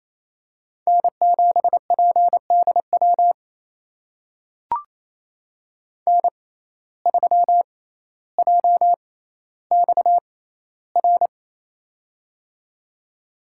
Morse code, heard as a keyed tone, about 28 wpm, spaced as for 14 wpm, 700 Hz. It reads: N7PDW N3JXR
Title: USA Call Signs - 5 Characters - 6x Character Spacing - 28wpm